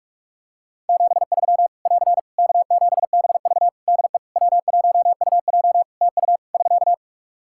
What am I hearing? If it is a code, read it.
Morse code, heard as a keyed tone, about 41 wpm, 700 Hz. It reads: 73 <AR> K7BV DE W1UJ TU <SK>
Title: K1USN SST QSO Practice 45wpm